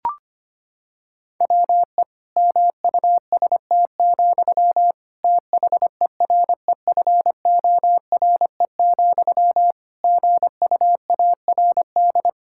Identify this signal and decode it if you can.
Morse code, heard as a keyed tone, about 25 wpm, 700 Hz. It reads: WE MUST, THEREFORE, GUARD
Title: Sentences from Top 1600 Words 25wpm